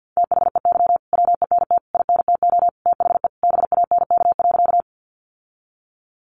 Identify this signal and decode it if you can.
Morse code, heard as a keyed tone, about 50 wpm, 700 Hz. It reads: THEY WENT INTO THE BANK.
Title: Sentences from Top 900 Words 50wpm